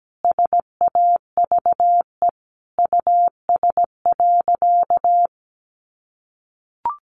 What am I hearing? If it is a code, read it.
Morse code, heard as a keyed tone, about 17 wpm, 700 Hz. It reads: SAVE US.
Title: Sentences from Top 1100 Words 17wpm